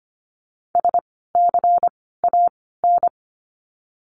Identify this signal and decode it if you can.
Morse code, heard as a keyed tone, about 25 wpm, spaced as for 16 wpm, 700 Hz. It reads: SCAN
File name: course / 25:10wpm / 094 Lesson 16f - Letter c - Words 25wpm